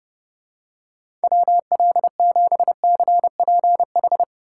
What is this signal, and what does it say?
Morse code, heard as a keyed tone, about 30 wpm, 700 Hz. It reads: WL7CPH